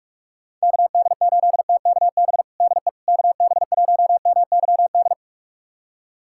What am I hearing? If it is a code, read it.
Morse code, heard as a keyed tone, about 46 wpm, 700 Hz. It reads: KD8TKB DE KB1MYD
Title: USA Callsign DE Callsign - 6 Characters - 45wpm